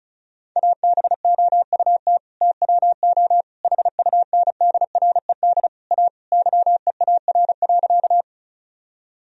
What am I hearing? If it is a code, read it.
Morse code, heard as a keyed tone, about 36 wpm, 700 Hz. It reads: ABOUT TWO HUNDRED A YEAR.
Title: Sentences from Top 1000 Words - Review Part 2 - 35wpm